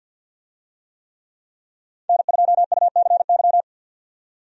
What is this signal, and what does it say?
Morse code, heard as a keyed tone, about 47 wpm, 700 Hz. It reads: N2UCY
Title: USA Call Signs - 5 Characters - 50wpm